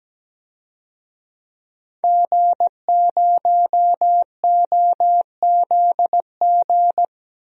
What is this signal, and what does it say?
Morse code, heard as a keyed tone, about 17 wpm, 700 Hz. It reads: G0OZG
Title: DXCC-100 Call Signs - 5 Characters - 17wpm